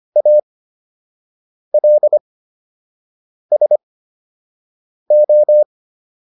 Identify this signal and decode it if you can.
Morse code, heard as a keyed tone, about 25 wpm, spaced as for 6 wpm, 600 Hz. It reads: ALSO